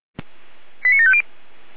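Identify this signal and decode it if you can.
Five-tone selective call (ZVEI), heard as quick runs of five tones, tones of about 70 ms. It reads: 89750
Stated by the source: ZVEI five-tone selective call received on an SDR.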